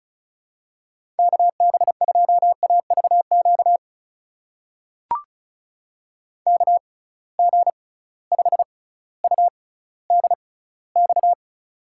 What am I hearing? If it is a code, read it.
Morse code, heard as a keyed tone, about 36 wpm, spaced as for 18 wpm, 690 Hz. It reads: KB2AVQ KG5UDX
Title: USA Call Signs - 6 Characters - 6x Character Spacing - 35wpm